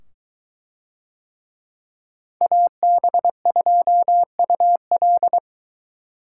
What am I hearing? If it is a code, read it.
Morse code, heard as a keyed tone, about 22 wpm, 700 Hz. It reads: AB2UL